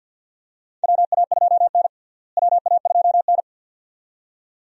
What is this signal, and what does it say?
Morse code, heard as a keyed tone, about 45 wpm, 700 Hz. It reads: WA2N WA2N
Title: USA Call Signs - 4 Characters - Repeated 2x - 50wpm